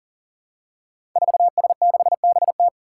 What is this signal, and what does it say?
Morse code, heard as a keyed tone, about 38 wpm, 700 Hz. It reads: 4S6BT
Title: Five Challenging Letter-Number Combinations 40wpm